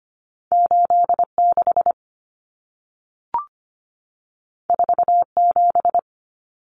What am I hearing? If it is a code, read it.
Morse code, heard as a keyed tone, about 25 wpm, 700 Hz. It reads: 86 47
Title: Numbers - 2-Digit 25wpm